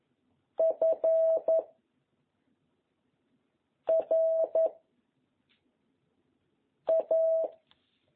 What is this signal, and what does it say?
Morse code, heard as a keyed tone, about 11 wpm, 660 Hz. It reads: F R A
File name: jh8cbh / 2021cw